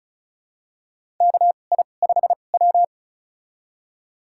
Morse code, heard as a keyed tone, about 34 wpm, spaced as for 25 wpm, 700 Hz. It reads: KI5W
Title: USA Call Signs - 4 Characters - 2x Character Spacing - 35wpm